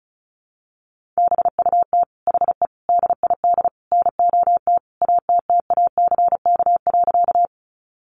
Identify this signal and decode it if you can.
Morse code, heard as a keyed tone, about 34 wpm, 700 Hz. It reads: BUT HE DID NOT ATTACK.
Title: Sentences from Top 1500 Words 35wpm